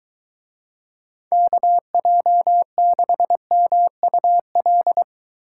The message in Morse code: KJ6MUL

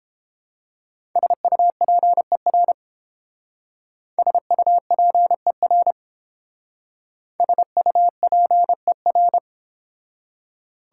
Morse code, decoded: SUPER SUPER SUPER

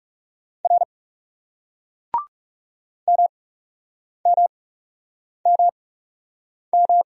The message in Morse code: R MMMM

